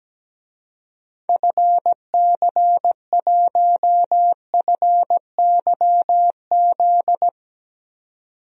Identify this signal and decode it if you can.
Morse code, heard as a keyed tone, about 17 wpm, 700 Hz. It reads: FC1FYZ
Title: DXCC-100 Call Signs - 6 Characters - 17wpm